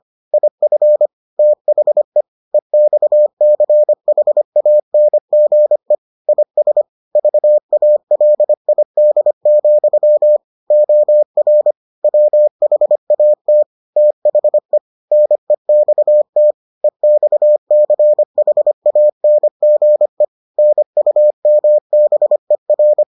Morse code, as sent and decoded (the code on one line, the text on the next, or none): .. ..-. / - .... / . -..- -.-. .... .- -. --. / .. ... / ...- .- .-.. .. -.. --..-- / --- .-. / .-- .... .- - / - .... / -. -..- - / . -..- -.-. .... .- -. --. / -. ..- -- -... .-.
IF THE EXCHANGE IS VALID, OR WHAT THE NEXT EXCHANGE NUMBER